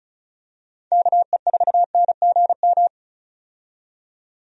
Morse code, heard as a keyed tone, about 34 wpm, 700 Hz. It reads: KE4NGM
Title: USA Call Signs - 6 Characters - 35wpm